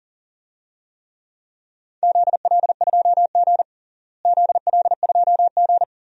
Morse code, heard as a keyed tone, about 40 wpm, 700 Hz. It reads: ZL2G ZL2G
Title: DXCC-100 Call Signs - 4 Characters - Repeated 3x - 40wpm